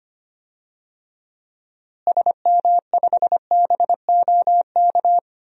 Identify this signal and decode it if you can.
Morse code, heard as a keyed tone, about 25 wpm, 700 Hz. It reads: SM5BOK
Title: DXCC-100 Call Signs - 6 Characters - Repeated 3x - 25wpm